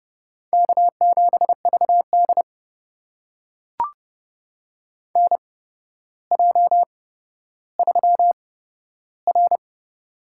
Morse code, heard as a keyed tone, about 30 wpm, spaced as for 13 wpm, 700 Hz. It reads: K7VD NJ3R
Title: USA Call Signs - 4 Characters - 8x Character Spacing - 30wpm